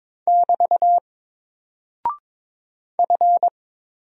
Morse code, heard as a keyed tone, about 22 wpm, 700 Hz. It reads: <BT> F
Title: Single Character (Letter:Number:Punctuation:Prosign) 22wpm